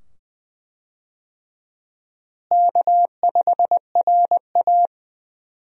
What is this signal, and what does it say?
Morse code, heard as a keyed tone, about 20 wpm, 700 Hz. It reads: K5RA
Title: USA Call Signs - 4 Characters - Repeated 2x - 20wpm